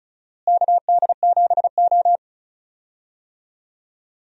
Morse code, heard as a keyed tone, about 36 wpm, 700 Hz. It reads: KD7O
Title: USA Call Signs - 4 Characters - Repeated 3x - 35wpm